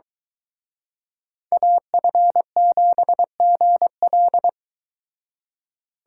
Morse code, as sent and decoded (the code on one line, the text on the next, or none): .- ..-. --... --. .-..
AF7GL